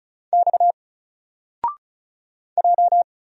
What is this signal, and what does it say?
Morse code, heard as a keyed tone, about 36 wpm, 700 Hz. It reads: X J
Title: Single Character (Letter:Number:Punctuation:Prosign) 35wpm